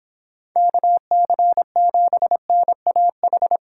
Morse code, heard as a keyed tone, about 27 wpm, 700 Hz. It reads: KC7NAH
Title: Speed Racing - USA 2x3 Call Signs 22wpm